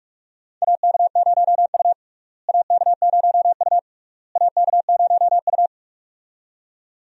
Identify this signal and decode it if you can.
Morse code, heard as a keyed tone, about 42 wpm, 700 Hz. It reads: AK0U AK0U AK0U
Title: USA Call Signs - 4 Characters - Repeated 3x - 45wpm